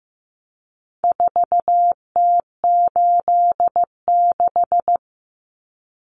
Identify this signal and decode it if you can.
Morse code, heard as a keyed tone, about 15 wpm, 700 Hz. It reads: 4T86
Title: Common Cut Numbers - 4-Digit 15wpm